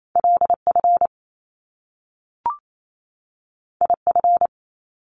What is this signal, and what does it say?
Morse code, heard as a keyed tone, about 28 wpm, 700 Hz. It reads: LF IF
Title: Top Two Letter Combinations in Words 28wpm